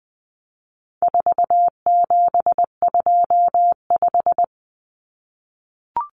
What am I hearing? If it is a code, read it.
Morse code, heard as a keyed tone, about 20 wpm, 700 Hz. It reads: 4725
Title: Numbers - 4-Digit 20wpm